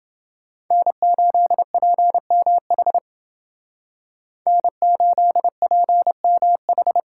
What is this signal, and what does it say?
Morse code, heard as a keyed tone, about 29 wpm, 700 Hz. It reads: N8PMH N8PMH